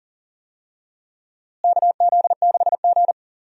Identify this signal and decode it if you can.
Morse code, heard as a keyed tone, about 40 wpm, 700 Hz. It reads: KZ6G
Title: USA Call Signs - 4 Characters - Repeated 2x - 40wpm